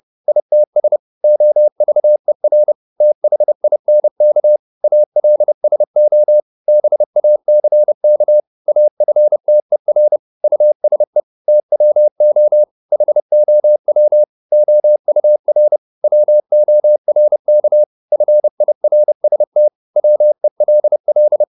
Morse code, heard as a keyed tone, about 30 wpm, 600 Hz. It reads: ITS OVER THINK ALSO BACK AFTER USE TWO HOW OUR WORK FIRST WELL
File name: Top-100-Words-30